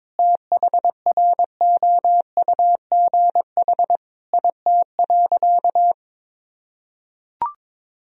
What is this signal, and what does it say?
Morse code, heard as a keyed tone, about 22 wpm, 700 Hz. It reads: THROUGH IT.
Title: Sentences from Top 1000 Words - Review Part 1 - 22wpm